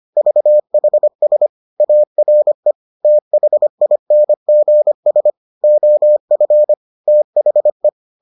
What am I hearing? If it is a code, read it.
Morse code, heard as a keyed tone, about 25 wpm, 600 Hz. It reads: VHS ARE THINGS OF THE